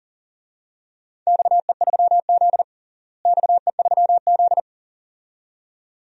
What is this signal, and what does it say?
Morse code, heard as a keyed tone, about 40 wpm, 700 Hz. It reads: XE3Z XE3Z
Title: DXCC-100 Call Signs - 4 Characters - Repeated 2x - 40wpm